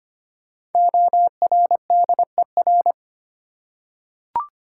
Morse code, heard as a keyed tone, about 25 wpm, 700 Hz. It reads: ORDER
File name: 5 Letter Words - First 3 Letters Given - 25wpm